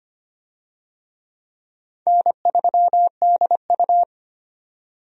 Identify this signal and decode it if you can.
Morse code, heard as a keyed tone, about 25 wpm, 700 Hz. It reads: N3DU
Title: USA Call Signs - 4 Characters - 25wpm